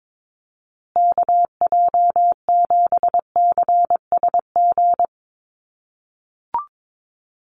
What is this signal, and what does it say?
Morse code, heard as a keyed tone, about 22 wpm, 700 Hz. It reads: KJ7CSG
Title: USA Call Signs Repeated 2x 22wpm